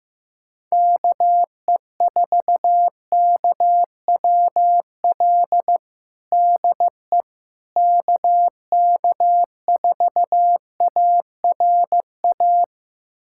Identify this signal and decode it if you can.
Morse code, heard as a keyed tone, about 15 wpm, 700 Hz. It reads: KE4KWL DE KK4ARA
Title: USA Callsign DE Callsign - 6 Characters - 15wpm